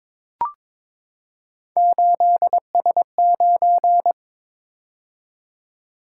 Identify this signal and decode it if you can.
Morse code, heard as a keyed tone, about 22 wpm, 700 Hz. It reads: 8S9
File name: practice / 22wpm / Three Letter-Number Combinations - 22wpm